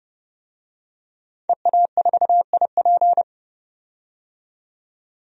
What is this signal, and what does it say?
Morse code, heard as a keyed tone, about 31 wpm, 700 Hz. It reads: EA4IP
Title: DXCC-100 Call Signs - 5 Characters - 30wpm